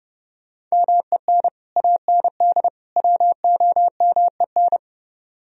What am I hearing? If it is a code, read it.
Morse code, heard as a keyed tone, about 37 wpm, 700 Hz. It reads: MEN AND WOMEN
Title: Speed-Racing - Sets of 3 Words 30wpm